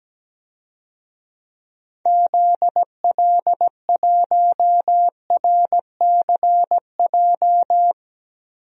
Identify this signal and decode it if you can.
Morse code, heard as a keyed tone, about 17 wpm, 700 Hz. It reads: ZL1RCJ
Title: DXCC-100 Call Signs - 6 Characters - Repeated 2x - 17wpm